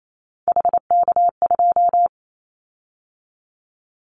Morse code, heard as a keyed tone, about 28 wpm, 700 Hz. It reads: HK2
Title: Callsign Prefixes for Top 100 DXCC - 28wpm